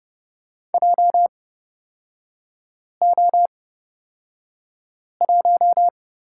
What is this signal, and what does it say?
Morse code, heard as a keyed tone, about 29 wpm, spaced as for 5 wpm, 700 Hz. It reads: JO1
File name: Lesson 36e - BT - Three Characters 30wpm